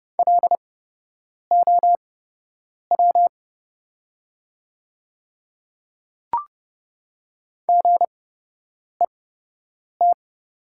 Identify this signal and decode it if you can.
Morse code, heard as a keyed tone, about 29 wpm, spaced as for 8 wpm, 700 Hz. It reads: LOW GET
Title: 8x Character Spacing - Top 200 Words - 30wpm